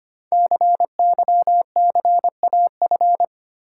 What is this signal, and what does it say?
Morse code, heard as a keyed tone, about 25 wpm, 690 Hz. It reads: CYCAF
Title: Code Groups - 5 Letters - 25wpm